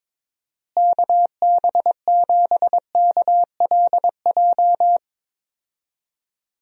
Morse code, heard as a keyed tone, about 22 wpm, 700 Hz. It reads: KB7KLJ
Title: USA Call Signs - 6 Characters - Repeated 2x - 22wpm